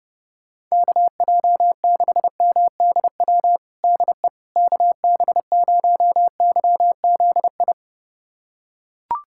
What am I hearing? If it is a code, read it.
Morse code, heard as a keyed tone, about 30 wpm, 700 Hz. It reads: KJ6MDW DE KB0YZI
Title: USA Callsign DE Callsign 30wpm